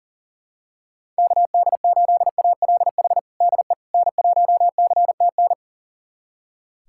Morse code, heard as a keyed tone, about 40 wpm, 700 Hz. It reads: KD8ALH DE N1CTN
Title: USA Callsign DE Callsign 40wpm